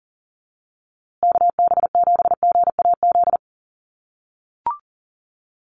Morse code, heard as a keyed tone, about 40 wpm, 700 Hz. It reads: KB7GAZ
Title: USA Call Signs Repeated 2x 40wpm